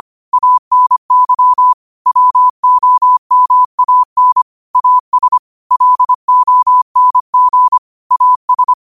Morse code, heard as a keyed tone, about 25 wpm, 1000 Hz. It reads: ANY WOMAN AS LONG AS
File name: qotdfiles / 25wpm